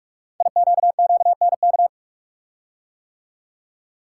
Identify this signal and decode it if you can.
Morse code, heard as a keyed tone, about 44 wpm, 700 Hz. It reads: IQQNK
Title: Code Groups - 5 Letters - 45wpm